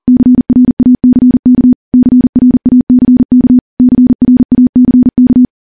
Morse code, heard as a keyed tone, about 40 wpm, 250 Hz. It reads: CRACK CRACK CRACK
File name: morse / morse-training-40WPM